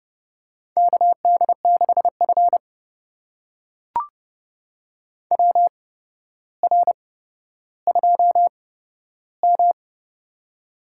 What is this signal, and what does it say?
Morse code, heard as a keyed tone, about 30 wpm, spaced as for 13 wpm, 700 Hz. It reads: KD6F WR2M
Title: USA Call Signs - 4 Characters - 8x Character Spacing - 30wpm